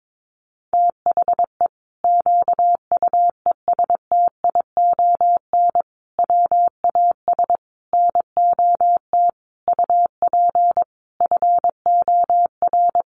Morse code, heard as a keyed tone, about 22 wpm, 700 Hz. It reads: THE QUESTION WAS NOT UP FOR
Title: Sentences from Top 1200 Words 22wpm